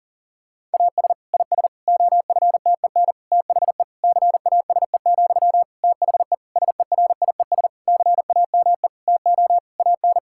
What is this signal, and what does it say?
Morse code, heard as a keyed tone, about 40 wpm, 700 Hz. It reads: AS IS OFTEN THE CASE, THE SERIES CAME TO AN
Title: Sentences from Top 1000 Words - Review Part 5 - 40wpm